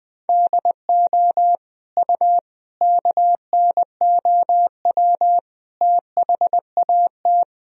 Morse code, heard as a keyed tone, about 20 wpm, 700 Hz. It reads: DO U KNOW THAT
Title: Sentences from Top 400 Words and Abbreviations 20wpm